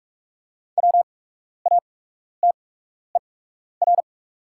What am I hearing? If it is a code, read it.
Morse code, heard as a keyed tone, about 43 wpm, spaced as for 12 wpm, 700 Hz. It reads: WATER
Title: 8x Character Spacing - Top 200 Words - 45wpm